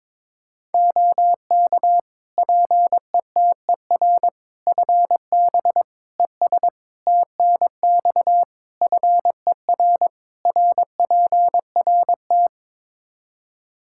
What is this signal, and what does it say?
Morse code, heard as a keyed tone, about 22 wpm, 700 Hz. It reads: OK PETER FB ES TNX FER RPRT
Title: K1USN SST QSO Practice 22wpm